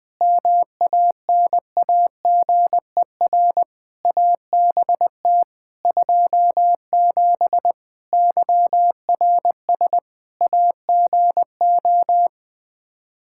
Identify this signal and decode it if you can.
Morse code, heard as a keyed tone, about 20 wpm, 700 Hz. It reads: MANAGER ABT 27 YRS AGO
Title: Practice QSO - Exchange 3 - More Info - 20wpm